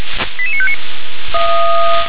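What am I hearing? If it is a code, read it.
Five-tone selective call (ZVEI), heard as quick runs of five tones, tones of about 70 ms. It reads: 88959
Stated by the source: ZVEI five-tone selective call 88841 received by SDR.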